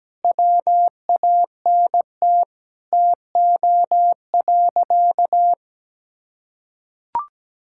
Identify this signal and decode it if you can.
Morse code, heard as a keyed tone, about 17 wpm, 700 Hz. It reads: WANT TO.
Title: Sentences from Top 1600 Words 17wpm